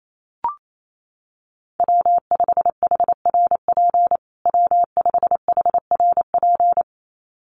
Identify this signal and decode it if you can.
Morse code, heard as a keyed tone, about 28 wpm, 700 Hz. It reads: W5HRP W5HRP
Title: USA Call Signs Repeated 2x 28wpm